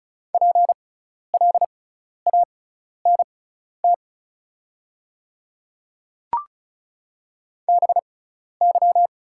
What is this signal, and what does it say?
Morse code, heard as a keyed tone, about 36 wpm, spaced as for 12 wpm, 690 Hz. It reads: PLANT BY
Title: 6x Character Spacing - Top 300 Words - 35wpm